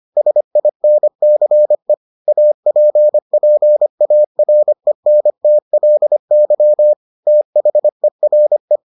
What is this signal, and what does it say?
Morse code, heard as a keyed tone, about 25 wpm, 600 Hz. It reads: SINCE APPARENTLY THERE